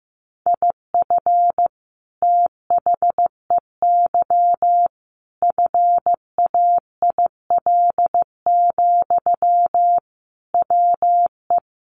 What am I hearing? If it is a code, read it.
Morse code, heard as a keyed tone, about 15 wpm, 700 Hz. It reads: IF THEY FAIL, WE
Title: Sentences from Top 1100 Words 15wpm